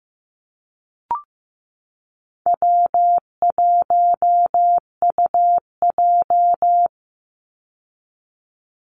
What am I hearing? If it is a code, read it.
Morse code, heard as a keyed tone, about 15 wpm, 700 Hz. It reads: W1UJ
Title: Call Signs Repeated 3x - 15wpm